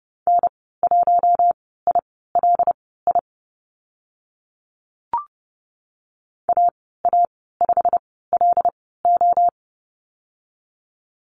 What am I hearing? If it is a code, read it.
Morse code, heard as a keyed tone, about 30 wpm, spaced as for 17 wpm, 700 Hz. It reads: N1ILI AA5LO
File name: Lesson 15g - Number 5 - Call Signs 30wpm